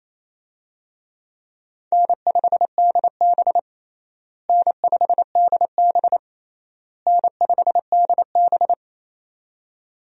Morse code, heard as a keyed tone, about 28 wpm, 700 Hz. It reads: N5DB N5DB N5DB
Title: USA Call Signs - 4 Characters - Repeated 3x - 28wpm